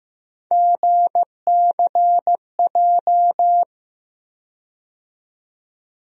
Morse code, heard as a keyed tone, about 15 wpm, 700 Hz. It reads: GCJ
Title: USA Call Sign Postfixes - 3 Letters 15wpm